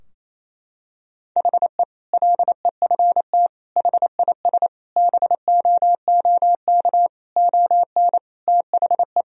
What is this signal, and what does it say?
Morse code, heard as a keyed tone, about 28 wpm, 700 Hz. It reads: HE LEFT HIS BOOK ON THE
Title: Sentences from Top 1000 Words - Review Part 2 - 28wpm